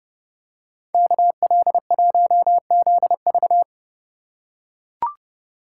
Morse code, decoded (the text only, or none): KL1ZV